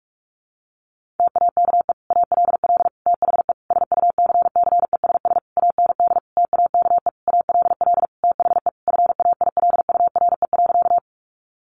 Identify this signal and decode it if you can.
Morse code, heard as a keyed tone, about 44 wpm, 700 Hz. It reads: TAKE ALL THE SUCCESS AND TAKE ALL THE FAILURE.
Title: Sentences from Top 900 Words 45wpm